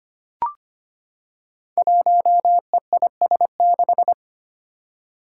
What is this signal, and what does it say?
Morse code, heard as a keyed tone, about 25 wpm, 700 Hz. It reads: 1EIS6
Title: Five Challenging Letter-Number Combinations 25wpm